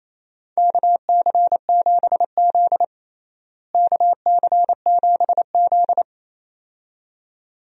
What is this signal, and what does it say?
Morse code, heard as a keyed tone, about 28 wpm, 700 Hz. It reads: KC7Z KC7Z
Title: USA Call Signs - 4 Characters - Repeated 2x - 28wpm